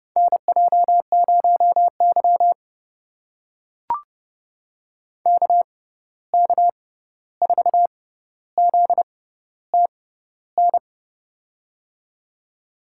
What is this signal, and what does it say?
Morse code, heard as a keyed tone, about 30 wpm, spaced as for 14 wpm, 700 Hz. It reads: NJ0Y KK4ZTN
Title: USA Call Signs - 6x Character Spacing - 30wpm